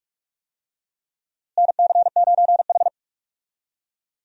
Morse code, heard as a keyed tone, about 45 wpm, 700 Hz. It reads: NC9H
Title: USA Call Signs - 4 Characters - Repeated 3x - 45wpm